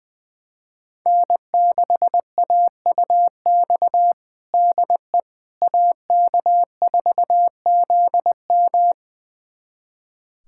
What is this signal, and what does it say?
Morse code, heard as a keyed tone, about 20 wpm, 700 Hz. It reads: N6AUX DE AK4ZM
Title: USA Callsign DE Callsign - 5 Characters - 20wpm